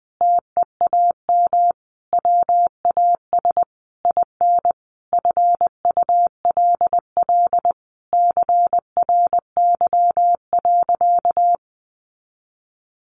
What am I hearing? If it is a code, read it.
Morse code, heard as a keyed tone, about 20 wpm, 700 Hz. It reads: TEAM WAS IN FULL CRY.
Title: Sentences from Top 1500 Words 20wpm